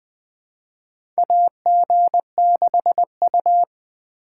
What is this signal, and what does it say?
Morse code, heard as a keyed tone, about 20 wpm, 700 Hz. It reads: AG6U
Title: USA Call Signs - 4 Characters - 4x Character Spacing - 20wpm